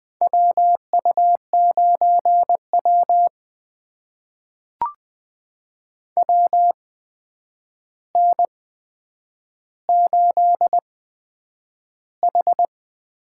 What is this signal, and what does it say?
Morse code, heard as a keyed tone, about 20 wpm, spaced as for 9 wpm, 700 Hz. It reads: WU9W WN8H
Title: USA Call Signs - 4 Characters - 8x Character Spacing - 20wpm